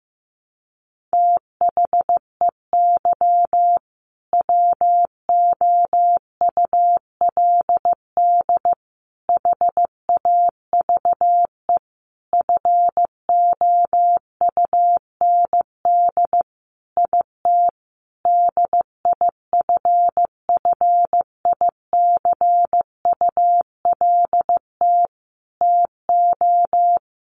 Morse code, heard as a keyed tone, about 15 wpm, 700 Hz. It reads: THEY WOULD HAVE FOUND IT DIFFICULT TO
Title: Sentences from Top 1400 Words 15wpm